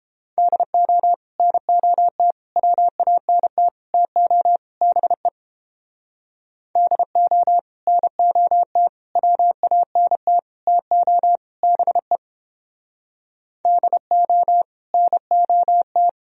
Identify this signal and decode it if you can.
Morse code, heard as a keyed tone, about 30 wpm, 700 Hz. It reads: DO NOT WANT TO BE DO NOT WANT TO BE DO NOT